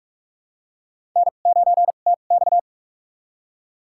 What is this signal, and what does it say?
Morse code, heard as a keyed tone, about 46 wpm, spaced as for 33 wpm, 700 Hz. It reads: N9TX
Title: USA Call Signs - 4 Characters - 2x Character Spacing - 45wpm